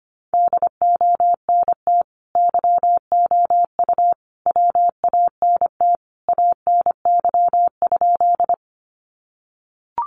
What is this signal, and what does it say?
Morse code, heard as a keyed tone, about 25 wpm, 700 Hz. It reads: DONT YOU WANT ANY?
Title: Sentences from Top 300 Words 25wpm